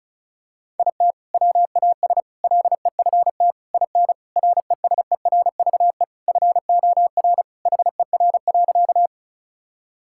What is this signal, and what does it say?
Morse code, heard as a keyed tone, about 34 wpm, 700 Hz. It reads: IT WAS LEFT IN RESERVE FOR HER.
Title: Sentences from Top 1000 Words - Review Part 6 - 35wpm